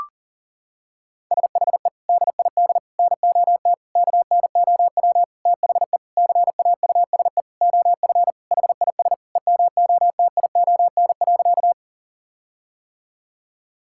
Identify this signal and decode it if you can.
Morse code, heard as a keyed tone, about 40 wpm, 700 Hz. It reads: SHE DID NOT KNOW THE CAUSE OF HIS EMOTION.
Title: Sentences from Top 1600 Words 40wpm